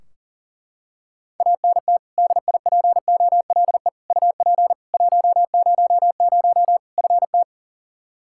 Morse code, heard as a keyed tone, about 37 wpm, 700 Hz. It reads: ANT DIPOLE UP 100 FT
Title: Practice QSO - Exchange 2 - Info - 40wpm